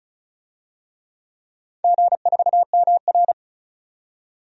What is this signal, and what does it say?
Morse code, heard as a keyed tone, about 34 wpm, 700 Hz. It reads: G4MR
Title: DXCC-100 Call Signs - 4 Characters - 35wpm